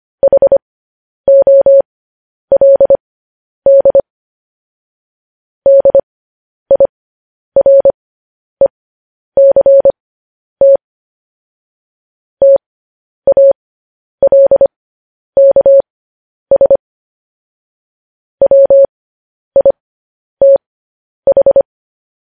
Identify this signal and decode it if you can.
Morse code, heard as a keyed tone, about 25 wpm, spaced as for 10 wpm, 550 Hz. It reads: HOLD DIRECT TALKS WITH
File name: podcast / News headlines-1x-25x10